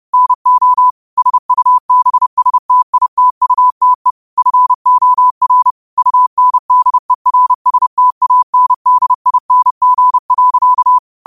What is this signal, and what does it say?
Morse code, heard as a keyed tone, about 30 wpm, 1000 Hz. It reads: NO SUBSTITUTE FOR UNDERSTANDING.
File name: 30wpm